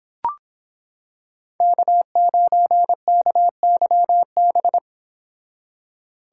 Morse code, heard as a keyed tone, about 26 wpm, 700 Hz. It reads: K9KYB